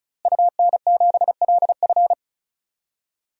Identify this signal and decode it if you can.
Morse code, heard as a keyed tone, about 36 wpm, 700 Hz. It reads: UN7LF
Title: DXCC-100 Call Signs - 5 Characters - Repeated 3x - 35wpm